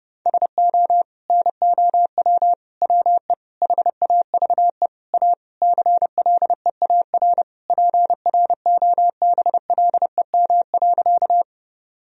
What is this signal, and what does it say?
Morse code, heard as a keyed tone, about 32 wpm, 700 Hz. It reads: SO NOW WE HAVE A CLEAR PROBLEM.
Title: Sentences from Top 1000 Words - Review Part 2 - 30wpm